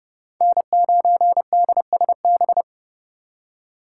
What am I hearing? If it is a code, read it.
Morse code, heard as a keyed tone, about 32 wpm, 700 Hz. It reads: N9DSB
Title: USA Call Signs - 5 Characters - 3x Character Spacing - 30wpm